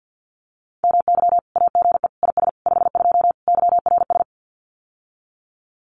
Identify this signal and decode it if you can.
Morse code, heard as a keyed tone, about 50 wpm, 700 Hz. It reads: MY AGE IS 52 YRS
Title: Common QSO Phrases Encore 50wpm